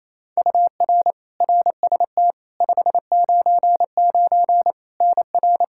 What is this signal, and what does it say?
Morse code, heard as a keyed tone, about 28 wpm, 700 Hz. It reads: UR RST 599 NR
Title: Practice QSO - Exchange 1 - Beginning - 28wpm